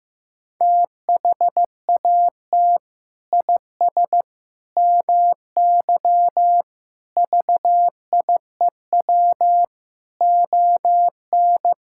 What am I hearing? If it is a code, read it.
Morse code, heard as a keyed tone, about 16 wpm, 700 Hz. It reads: THAT IS MY VIEW ON
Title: Sentences from Top 1000 Words - Review Part 6 - 15wpm